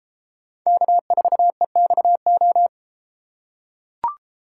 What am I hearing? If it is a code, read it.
Morse code, heard as a keyed tone, about 32 wpm, 700 Hz. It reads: K4EXO